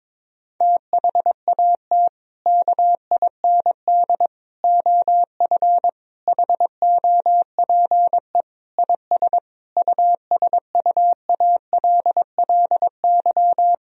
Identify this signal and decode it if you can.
Morse code, heard as a keyed tone, about 22 wpm, 700 Hz. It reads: THAT KIND OF HOPE IS USUALLY
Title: Sentences from Top 1000 Words - Review Part 3 - 22wpm